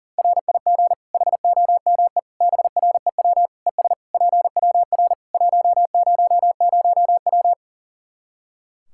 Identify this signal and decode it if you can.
Morse code, heard as a keyed tone, about 37 wpm, 700 Hz. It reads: RIG HOME BREW ES PWR 100W
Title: Practice QSO - Exchange 2 - Info - 40wpm